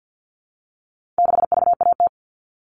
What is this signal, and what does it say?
Morse code, heard as a keyed tone, about 50 wpm, 700 Hz. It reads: 6VAT
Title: All Cut Numbers - 4-Digit 50wpm